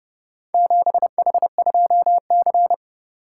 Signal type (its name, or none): Morse code